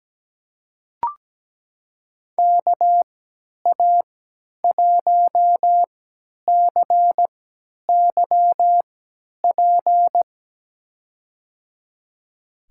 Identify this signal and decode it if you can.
Morse code, heard as a keyed tone, about 17 wpm, spaced as for 10 wpm, 700 Hz. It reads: KA1CYP
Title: USA Call Signs - 6 Characters - 3x Character Spacing - 17wpm